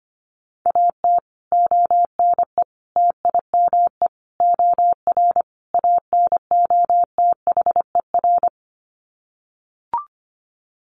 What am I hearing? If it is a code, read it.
Morse code, heard as a keyed tone, about 25 wpm, 700 Hz. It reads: AT ONE TIME OR ANOTHER
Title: Sets of 5 Words Encore 25wpm